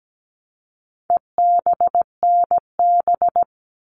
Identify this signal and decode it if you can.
Morse code, heard as a keyed tone, about 17 wpm, 700 Hz. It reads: EBNB
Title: All Cut Numbers - 4-Digit 17wpm